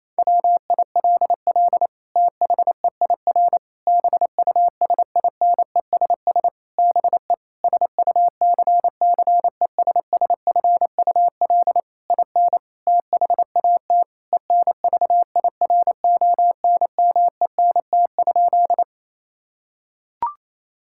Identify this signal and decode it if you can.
Morse code, heard as a keyed tone, about 28 wpm, 700 Hz. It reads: WILL THEIR BUSINESS BE SUCCESSFUL IN THAT ENVIRONMENT?
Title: Sentences from Top 1000 Words - Review Part 5 - 28wpm